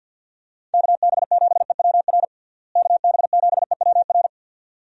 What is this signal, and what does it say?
Morse code, heard as a keyed tone, about 52 wpm, 700 Hz. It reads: KB7EWR KB7EWR
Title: USA Call Signs - 6 Characters - Repeated 2x - 50wpm